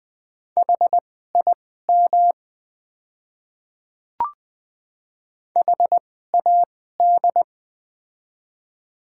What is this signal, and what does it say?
Morse code, heard as a keyed tone, about 20 wpm, spaced as for 14 wpm, 700 Hz. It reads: HIM HAD
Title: 2x Character Spacing - Top 100 Words - 20wpm